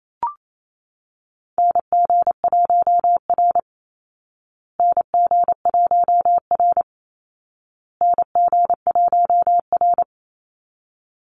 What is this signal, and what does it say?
Morse code, heard as a keyed tone, about 28 wpm, 700 Hz. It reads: NG1R NG1R NG1R
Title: Call Signs Repeated 2x - 28wpm